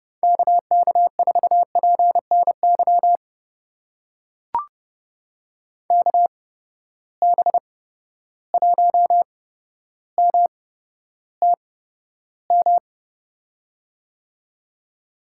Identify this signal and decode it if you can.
Morse code, heard as a keyed tone, about 30 wpm, spaced as for 13 wpm, 700 Hz. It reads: KK4PNY KB1MTM
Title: USA Call Signs - 6 Characters - 8x Character Spacing - 30wpm